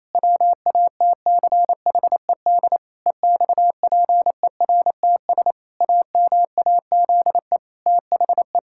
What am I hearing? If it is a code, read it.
Morse code, heard as a keyed tone, about 27 wpm, 700 Hz. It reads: WATCHED EXPERTS AMAZE THE
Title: Pangram Challenge v2 - 28wpm